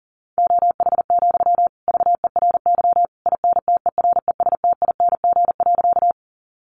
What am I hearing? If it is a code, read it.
Morse code, heard as a keyed tone, about 40 wpm, 700 Hz. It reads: OH, VERY INTERESTING.
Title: Sentences from Top 1000 Words 40wpm